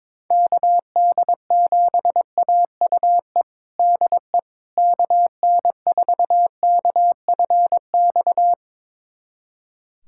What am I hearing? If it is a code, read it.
Morse code, heard as a keyed tone, about 22 wpm, 700 Hz. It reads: KD7AUE DE KN4KFX
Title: USA Callsign DE Callsign - 6 Characters - 22wpm